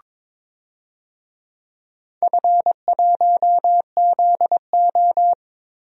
Morse code, heard as a keyed tone, about 22 wpm, 700 Hz. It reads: F1ZO
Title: DXCC-100 Call Signs - 4 Characters - Repeated 2x - 22wpm